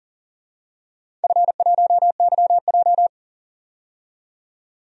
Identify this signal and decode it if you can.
Morse code, heard as a keyed tone, about 40 wpm, 700 Hz. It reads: F1YJ
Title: DXCC-100 Call Signs - 4 Characters - 40wpm